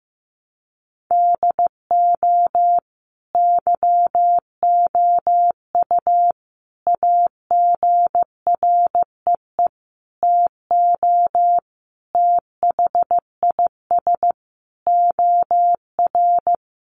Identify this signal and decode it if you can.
Morse code, heard as a keyed tone, about 15 wpm, 700 Hz. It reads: DO YOU AGREE TO THIS OR
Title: Sentences from Top 1100 Words 15wpm